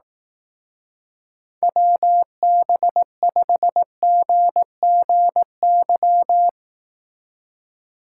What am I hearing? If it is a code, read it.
Morse code, heard as a keyed tone, about 18 wpm, 700 Hz. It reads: WB5GGY
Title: Speed Racing - USA 2x3 Call Signs 15wpm